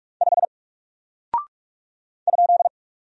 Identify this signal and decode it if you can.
Morse code, heard as a keyed tone, about 46 wpm, 700 Hz. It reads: 5 ?
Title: Single Character (Letter:Number:Punctuation:Prosign) Alt 45wpm